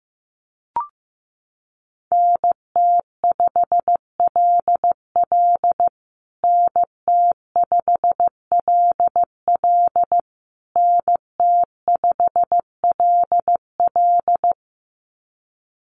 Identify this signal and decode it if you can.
Morse code, heard as a keyed tone, about 15 wpm, 700 Hz. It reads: NT5LL NT5LL NT5LL
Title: USA Call Signs Repeated 3x 15wpm